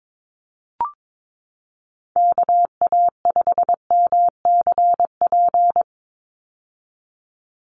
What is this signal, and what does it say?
Morse code, heard as a keyed tone, about 22 wpm, 700 Hz. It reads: KA5MCP
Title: USA Call Signs 22wpm